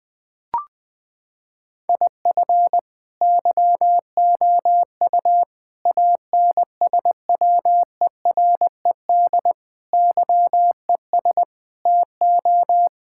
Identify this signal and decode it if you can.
Morse code, heard as a keyed tone, about 20 wpm, 700 Hz. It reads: IF YOU ANSWERED YES TO
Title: Sets of 9 Words - 20wpm